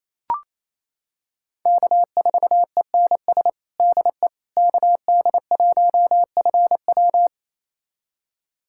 Morse code, heard as a keyed tone, about 28 wpm, 700 Hz. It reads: K4ENS DE KD1FW
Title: USA Callsign DE Callsign 28wpm